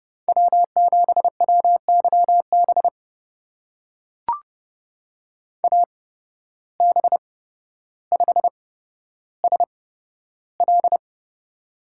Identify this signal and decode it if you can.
Morse code, heard as a keyed tone, about 30 wpm, spaced as for 13 wpm, 700 Hz. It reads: W7WYB AB5SL
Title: USA Call Signs - 5 Characters - 8x Character Spacing - 30wpm